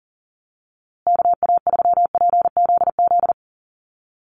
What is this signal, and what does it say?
Morse code, heard as a keyed tone, about 40 wpm, 700 Hz. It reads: KA3PZZ
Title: USA Call Signs Repeated 2x 40wpm